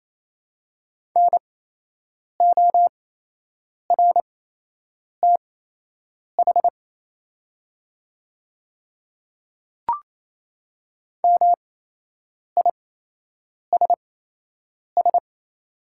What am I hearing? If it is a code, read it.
Morse code, heard as a keyed tone, about 28 wpm, spaced as for 8 wpm, 700 Hz. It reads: NORTH MISS
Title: 8x Character Spacing - Top 500 Words - 28wpm